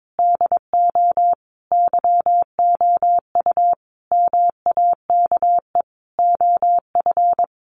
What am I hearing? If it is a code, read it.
Morse code, heard as a keyed tone, about 22 wpm, 700 Hz. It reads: DO YOU MAKE OF
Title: Sets of 5 Words 22wpm